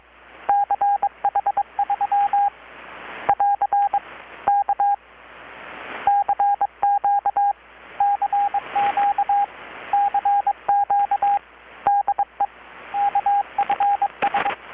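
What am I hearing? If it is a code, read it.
Morse code, heard as a keyed tone, about 21 wpm, 810 Hz, with radio noise. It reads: CH3 <AR> K CQ CQ CQ DE KFS
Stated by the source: Morse from KPH's Night of Nights XX maritime radiotelegraphy event, received in Chiapas, Mexico about 3,800 km away on a Kenwood R-600 with a grounded 10 m long-wire antenna.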